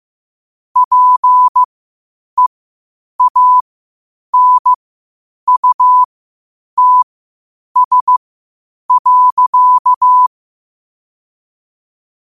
Morse code, heard as a keyed tone, about 15 wpm, spaced as for 8 wpm, 1000 Hz. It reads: PEANUTS.
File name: qotdfiles / 8wpm